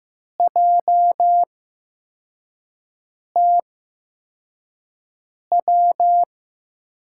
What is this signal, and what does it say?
Morse code, heard as a keyed tone, about 15 wpm, spaced as for 4 wpm, 700 Hz. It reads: JTW
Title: USA Call Signs - 5 Characters - 8x Character Spacing - 15wpm